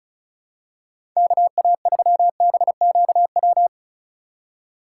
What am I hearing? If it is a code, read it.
Morse code, heard as a keyed tone, about 36 wpm, 700 Hz. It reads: KA3BQW
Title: USA Call Signs - 6 Characters - 4x Character Spacing - 35wpm